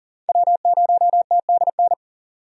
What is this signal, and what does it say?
Morse code, heard as a keyed tone, about 40 wpm, 700 Hz. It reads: W0TDN
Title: USA Call Signs - 5 Characters - 40wpm